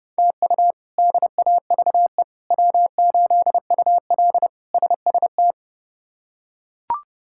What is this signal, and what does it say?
Morse code, heard as a keyed tone, about 30 wpm, 700 Hz. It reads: TU DAVE W8UL SST
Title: K1USN SST QSO Practice - 30wpm